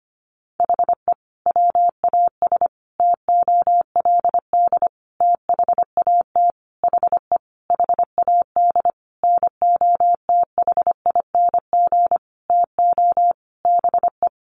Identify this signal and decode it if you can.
Morse code, heard as a keyed tone, about 25 wpm, 700 Hz. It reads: HE WAS TOLD THAT HE HAD NOTHING TO BE
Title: Sentences from Top 1500 Words 25wpm